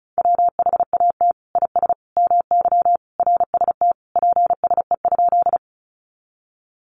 Morse code, heard as a keyed tone, about 34 wpm, 700 Hz. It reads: WHAT IS MY RST PSE?
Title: Common QSO Phrases 35wpm